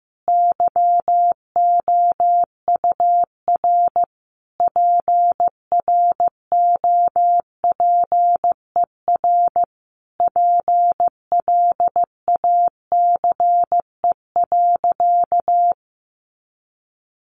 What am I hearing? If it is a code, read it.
Morse code, heard as a keyed tone, about 15 wpm, 700 Hz. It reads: YOUR PROPER PLACE.
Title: Sentences from Top 800 Words 15wpm